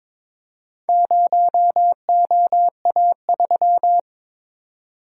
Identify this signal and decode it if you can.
Morse code, heard as a keyed tone, about 22 wpm, 700 Hz. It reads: 0OA3
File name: practice / 22wpm / Four Letter-Number Combinations - 22wpm